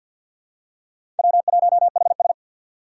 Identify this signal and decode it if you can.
Morse code, heard as a keyed tone, about 49 wpm, 700 Hz. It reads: W1HS
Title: USA Call Signs - 4 Characters - 2x Character Spacing - 50wpm